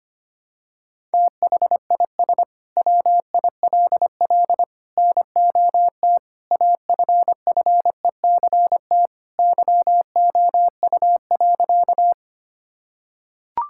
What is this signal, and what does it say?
Morse code, heard as a keyed tone, about 25 wpm, 700 Hz. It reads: THIS WILL NOT AFFECT YOU.
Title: Sentences from Top 1000 Words - Review Part 6 - 25wpm